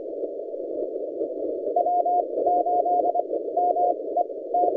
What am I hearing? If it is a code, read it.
Morse code, heard as a keyed tone, about 24 wpm, spaced as for 17 wpm, 710 Hz, with noise. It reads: W8MET
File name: posts / 3d2rh-20cw- -0330z